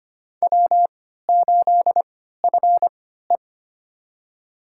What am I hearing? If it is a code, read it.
Morse code, heard as a keyed tone, about 25 wpm, spaced as for 14 wpm, 700 Hz. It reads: W8FE
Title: USA Call Signs - 4 Characters - 3x Character Spacing - 25wpm